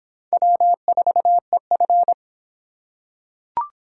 Morse code, heard as a keyed tone, about 25 wpm, 700 Hz. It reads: W4EF